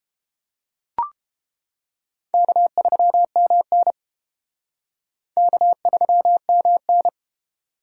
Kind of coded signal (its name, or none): Morse code